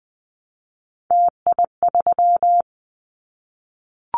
Morse code, heard as a keyed tone, about 20 wpm, 700 Hz. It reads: TI3
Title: Callsign Prefixes for Top 100 DXCC - 20wpm